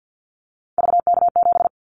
Morse code, heard as a keyed tone, about 50 wpm, 690 Hz. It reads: VK7
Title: Callsign Prefixes for Top 100 DXCC - 50wpm